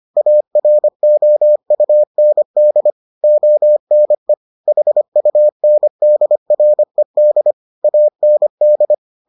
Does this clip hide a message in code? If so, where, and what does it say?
Yes: Morse code from 0.2 to 8.9 s, heard as a keyed tone, about 25 wpm, 600 Hz. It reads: AROUND ONE HUNDRED AND